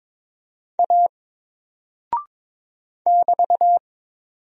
Morse code, heard as a keyed tone, about 22 wpm, 700 Hz. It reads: A <BT>
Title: Single Character (Letter:Number:Punctuation:Prosign) Alt 22wpm